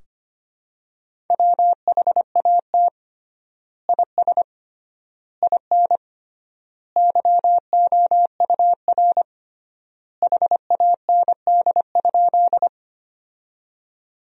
Morse code, heard as a keyed tone, about 25 wpm, 700 Hz. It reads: WHAT IS IN YOUR HAND?
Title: 3x Word-Spacing - Sentences from Top 200 Words - 25wpm